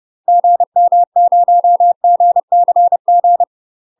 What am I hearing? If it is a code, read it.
Morse code, heard as a keyed tone, about 30 wpm, 700 Hz. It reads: GM0GCG
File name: UK MASTER QSOs TRAINING/30WPM